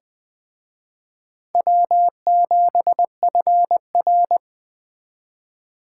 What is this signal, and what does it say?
Morse code, heard as a keyed tone, about 20 wpm, 700 Hz. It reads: W7FR